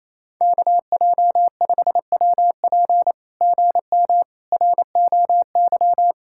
Morse code, heard as a keyed tone, about 27 wpm, 700 Hz. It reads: KJ5WP GM ROY
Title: SOTA QSO Practice 28wpm